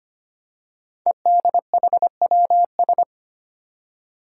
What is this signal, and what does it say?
Morse code, heard as a keyed tone, about 25 wpm, 700 Hz. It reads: EDHWS
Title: Five Challenging Letter-Number Combinations 25wpm